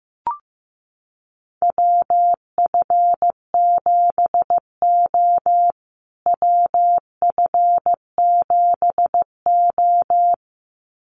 Morse code, heard as a keyed tone, about 15 wpm, 700 Hz. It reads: WF7O WF7O
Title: USA Call Signs Repeated 2x 15wpm